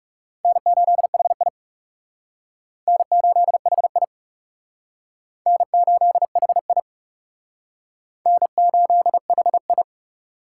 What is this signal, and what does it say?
Morse code, heard as a keyed tone, about 36 wpm, 700 Hz. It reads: N8HI N8HI N8HI N8HI
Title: Speed Racing - USA 1x2 Call Signs 30wpm